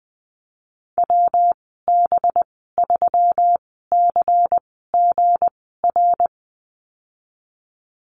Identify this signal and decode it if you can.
Morse code, heard as a keyed tone, about 20 wpm, spaced as for 15 wpm, 700 Hz. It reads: WB3CGR
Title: 186 Lesson 30g - Letter b - Call Signs 20wpm